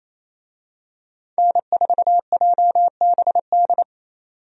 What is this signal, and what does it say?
Morse code, heard as a keyed tone, about 27 wpm, 700 Hz. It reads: N4JBD
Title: USA Call Signs - 5 Characters - 28wpm